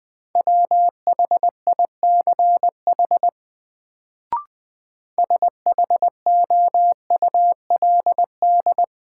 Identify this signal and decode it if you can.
Morse code, heard as a keyed tone, about 21 wpm, 700 Hz. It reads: WHICH SHOULD
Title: Sets of 2 Words - New Word Drill - 20wpm